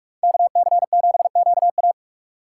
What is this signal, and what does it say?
Morse code, heard as a keyed tone, about 45 wpm, 700 Hz. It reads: KC7QA
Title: USA Call Signs - 5 Characters - 2x Character Spacing - 45wpm